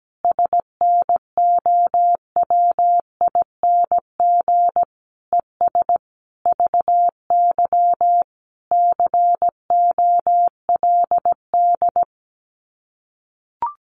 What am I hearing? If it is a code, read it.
Morse code, heard as a keyed tone, about 17 wpm, 700 Hz. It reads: SNOWING ES VY COLD
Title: Common QSO Phrases 17wpm